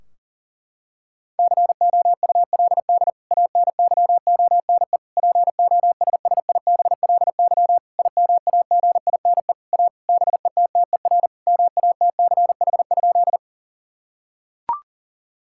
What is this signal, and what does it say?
Morse code, heard as a keyed tone, about 40 wpm, 700 Hz. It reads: COULD ANYONE POSSIBLY IMAGINE A BETTER MATCH?
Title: Sentences from Top 1600 Words 40wpm